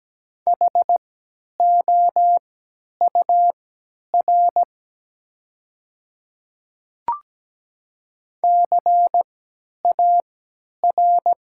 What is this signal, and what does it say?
Morse code, heard as a keyed tone, about 17 wpm, spaced as for 10 wpm, 700 Hz. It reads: HOUR CAR
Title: 3x Character Spacing - Top 500 Words - 17wpm